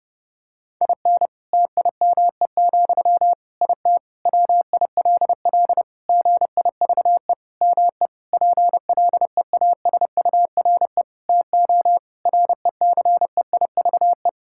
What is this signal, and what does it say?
Morse code, heard as a keyed tone, about 30 wpm, 700 Hz. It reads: IN TIME, IT WILL GIVE ME PLEASURE TO RECEIVE
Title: Sentences from Top 1600 Words 30wpm